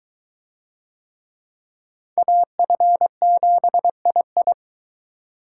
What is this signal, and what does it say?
Morse code, heard as a keyed tone, about 22 wpm, 700 Hz. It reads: AF7II